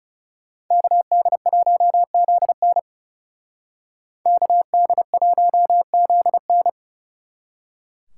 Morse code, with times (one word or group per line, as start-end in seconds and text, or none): 0.7-2.8 KD1ZN
4.3-6.7 KD1ZN